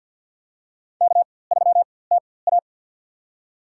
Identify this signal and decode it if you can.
Morse code, heard as a keyed tone, about 47 wpm, spaced as for 23 wpm, 700 Hz. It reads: K3TA
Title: USA Call Signs - 4x Character Spacing - 50wpm